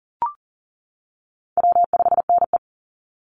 Morse code, heard as a keyed tone, about 40 wpm, 700 Hz. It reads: W5NE
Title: Call Signs - 40wpm